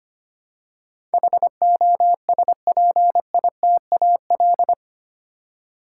Morse code, heard as a keyed tone, about 25 wpm, 700 Hz. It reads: HOSPITAL
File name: Eight Letter Words - 25wpm